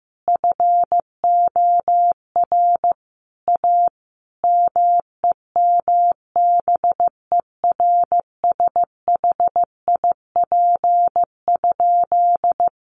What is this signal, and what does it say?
Morse code, heard as a keyed tone, about 14 wpm, 700 Hz. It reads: FOR A MEMBERSHIP?
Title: Sentences from Top 1400 Words 15wpm